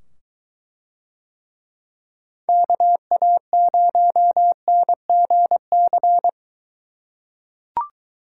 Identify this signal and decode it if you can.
Morse code, heard as a keyed tone, about 21 wpm, 700 Hz. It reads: KA0NGC